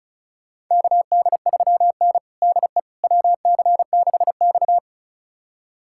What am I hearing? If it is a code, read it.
Morse code, heard as a keyed tone, about 36 wpm, 700 Hz. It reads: KD3N DE WC6X
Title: USA Callsign DE Callsign - 4 Characters - 35wpm